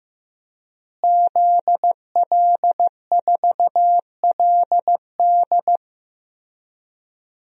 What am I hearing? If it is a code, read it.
Morse code, heard as a keyed tone, about 15 wpm, 700 Hz. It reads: ZL4LD
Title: DXCC-100 Call Signs - 5 Characters - Repeated 3x - 15wpm